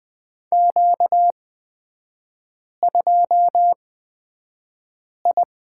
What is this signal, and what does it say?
Morse code, heard as a keyed tone, about 20 wpm, spaced as for 5 wpm, 700 Hz. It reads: Q2I
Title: 250 Lesson 40e - BK - Three Characters 20wpm